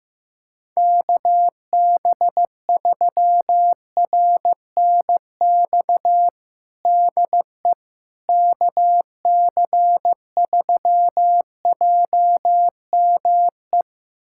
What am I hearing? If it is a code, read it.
Morse code, heard as a keyed tone, about 15 wpm, 700 Hz. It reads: KB3RNX DE KC3JME
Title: USA Callsign DE Callsign 15wpm